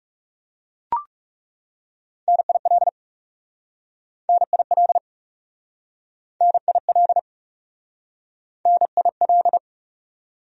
Morse code, heard as a keyed tone, about 40 wpm, 700 Hz. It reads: NIL NIL NIL NIL